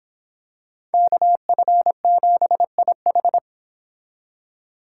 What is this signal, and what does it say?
Morse code, heard as a keyed tone, about 26 wpm, 700 Hz. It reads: KF7IH